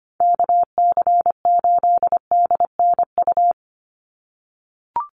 Morse code, heard as a keyed tone, about 25 wpm, 700 Hz. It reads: KC8DNU
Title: USA Call Signs 25wpm